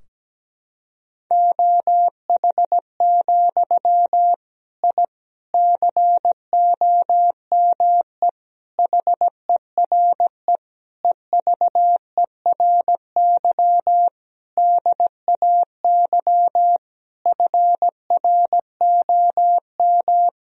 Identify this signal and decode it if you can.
Morse code, heard as a keyed tone, about 17 wpm, 700 Hz. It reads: OH, I COME HERE EVERY DAY FROM
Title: Sentences from Top 1000 Words - Review Part 2 - 17wpm